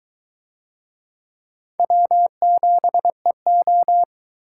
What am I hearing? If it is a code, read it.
Morse code, heard as a keyed tone, about 21 wpm, 700 Hz. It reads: W7EO